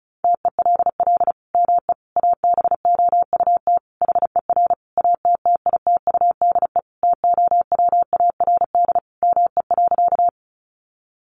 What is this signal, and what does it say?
Morse code, heard as a keyed tone, about 34 wpm, 700 Hz. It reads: TELL ME ABOUT HER ATTITUDE TOWARD ME.
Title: Sentences from Top 1300 Words 35wpm